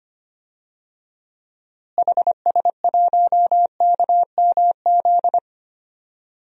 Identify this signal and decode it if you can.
Morse code, heard as a keyed tone, about 25 wpm, 700 Hz. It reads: HS1KMZ
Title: DXCC-100 Call Signs - 6 Characters - Repeated 2x - 25wpm